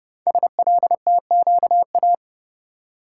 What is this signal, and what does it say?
Morse code, heard as a keyed tone, about 30 wpm, 700 Hz. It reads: SLTQA
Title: Code Groups - 5 Letters - 30wpm